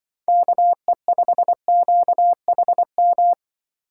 Morse code, heard as a keyed tone, about 23 wpm, 700 Hz. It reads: KE5QHM
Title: Speed Racing - USA 2x3 Call Signs 20wpm